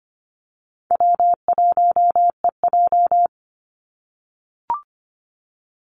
Morse code, heard as a keyed tone, about 25 wpm, 690 Hz. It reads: W1EJ
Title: NA SOTA Activator Callsigns 25wpm